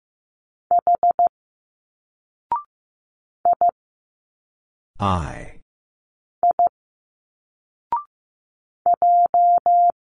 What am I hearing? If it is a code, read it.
Morse code, heard as a keyed tone, about 15 wpm, 700 Hz. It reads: H I I J
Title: Single Letters 15wpm